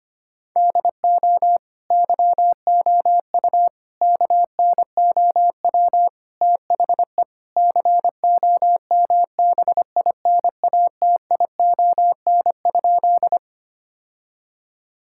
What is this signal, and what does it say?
Morse code, heard as a keyed tone, about 25 wpm, 700 Hz. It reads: DO YOU KNOW THE COMBINATION?
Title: Sentences from Top 1000 Words - Review Part 6 - 25wpm